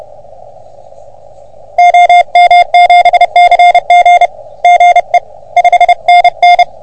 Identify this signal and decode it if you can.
Morse code, heard as a keyed tone, about 30 wpm, 710 Hz. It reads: OM7CG GE 5NN
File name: OM3PA PILUP CUC 388